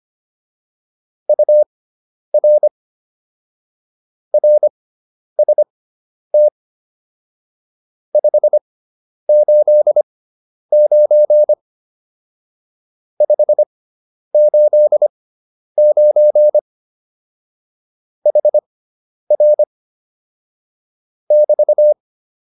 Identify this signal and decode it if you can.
Morse code, heard as a keyed tone, about 25 wpm, spaced as for 10 wpm, 600 Hz. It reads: UR RST 589 589 HR <BT>
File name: QSO207 10